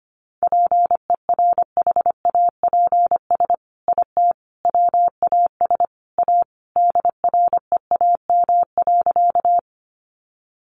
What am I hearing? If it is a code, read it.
Morse code, heard as a keyed tone, about 25 wpm, 700 Hz. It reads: PERHAPS IT WAS A DREAM.
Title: Sentences from Top 1500 Words 25wpm